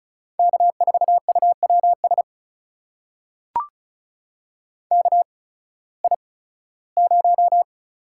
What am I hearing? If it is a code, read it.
Morse code, heard as a keyed tone, about 36 wpm, spaced as for 19 wpm, 700 Hz. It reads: K4UWS KI0
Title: USA Call Signs - 8x Character Spacing - 35wpm